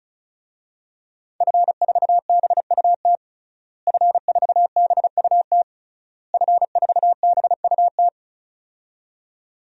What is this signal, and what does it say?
Morse code, heard as a keyed tone, about 36 wpm, 700 Hz. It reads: F4BUT F4BUT F4BUT
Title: DXCC-100 Call Signs - 5 Characters - Repeated 3x - 35wpm